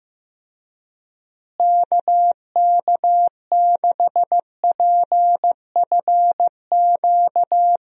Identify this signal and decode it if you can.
Morse code, heard as a keyed tone, about 15 wpm, 700 Hz. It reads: KK6PFQ
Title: USA Call Signs - 6 Characters - Repeated 3x - 15wpm